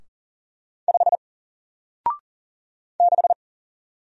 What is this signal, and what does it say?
Morse code, heard as a keyed tone, about 39 wpm, 700 Hz. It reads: 5 6
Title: Single Letter-Number Alt 40wpm